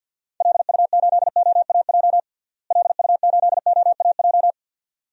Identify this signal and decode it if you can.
Morse code, heard as a keyed tone, about 51 wpm, 700 Hz. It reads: LU8OAJ LU8OAJ
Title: DXCC-100 Call Signs - 6 Characters - Repeated 2x - 50wpm